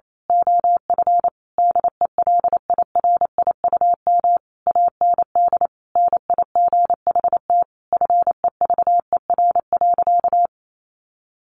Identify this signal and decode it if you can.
Morse code, heard as a keyed tone, about 28 wpm, 700 Hz. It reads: OF DELIRIUM AND NIGHT FEVER.
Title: A Sound Of Thunder 28wpm